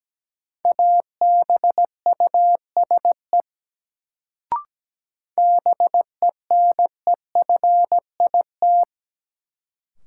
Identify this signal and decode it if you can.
Morse code, heard as a keyed tone, about 17 wpm, 700 Hz. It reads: ABUSE BENEFIT
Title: Sentences from Top 1300 Words - New Word Drill Alt - 17wpm